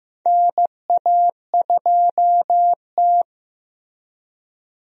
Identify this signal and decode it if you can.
Morse code, heard as a keyed tone, about 15 wpm, 700 Hz. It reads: NA2T
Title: USA Call Signs - 4 Characters - Repeated 2x - 15wpm